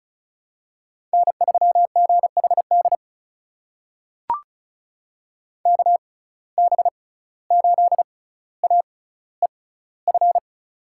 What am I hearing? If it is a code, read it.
Morse code, heard as a keyed tone, about 35 wpm, spaced as for 17 wpm, 700 Hz. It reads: N3GHD KB8AEF
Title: USA Call Signs - 6x Character Spacing - 35wpm